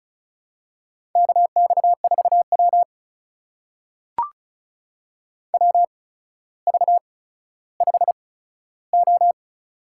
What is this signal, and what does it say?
Morse code, heard as a keyed tone, about 36 wpm, spaced as for 15 wpm, 700 Hz. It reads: KX4W WV5O
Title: USA Call Signs - 4 Characters - 8x Character Spacing - 35wpm